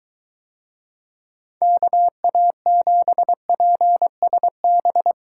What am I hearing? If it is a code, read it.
Morse code, heard as a keyed tone, about 23 wpm, 700 Hz. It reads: KA7PSB